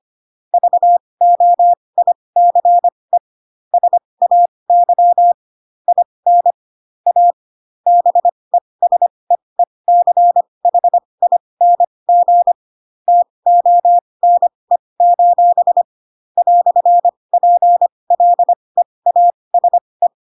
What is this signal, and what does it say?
Morse code, heard as a keyed tone, about 25 wpm, spaced as for 20 wpm, 700 Hz. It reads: VOICE SAY IN A BESEECHING TONE: "PLEASE
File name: Pinocchio / Pinocchio ch1 15 25 ik1hge